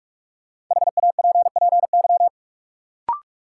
Morse code, heard as a keyed tone, about 45 wpm, 700 Hz. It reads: HAPPY
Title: Sentences from Top 1000 Words - New Word Drill - 45wpm